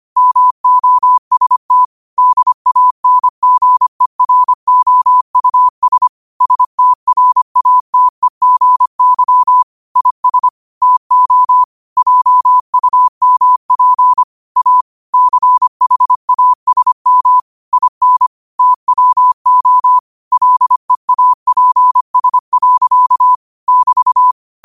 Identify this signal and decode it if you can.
Morse code, heard as a keyed tone, about 25 wpm, 1000 Hz. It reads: MOST DANGEROUS STRATEGY IS TO JUMP A CHASM IN TWO LEAPS. <BT>